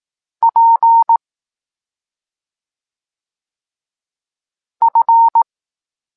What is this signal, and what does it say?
Morse code, heard as a keyed tone, about 18 wpm, 920 Hz. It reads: P F
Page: Tekst 25 – Nauka liter P F